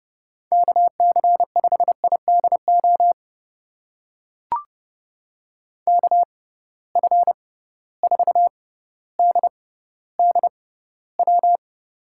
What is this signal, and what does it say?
Morse code, heard as a keyed tone, about 29 wpm, spaced as for 15 wpm, 700 Hz. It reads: KC5IDO KF4DDW
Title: USA Call Signs - 6 Characters - 6x Character Spacing - 30wpm